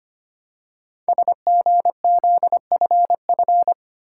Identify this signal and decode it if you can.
Morse code, heard as a keyed tone, about 25 wpm, 700 Hz. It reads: SGZFF
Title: Code Groups - 5 Letters - 25wpm